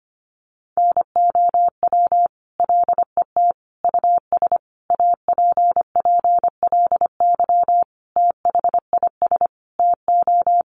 Morse code, heard as a keyed tone, about 25 wpm, 700 Hz. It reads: NOW LET US APPLY THIS TO
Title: Sentences from Top 1200 Words 25wpm